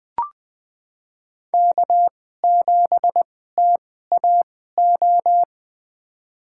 Morse code, heard as a keyed tone, about 20 wpm, spaced as for 14 wpm, 700 Hz. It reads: K7TAO
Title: USA Call Signs - 5 Characters - 2x Character Spacing - 20wpm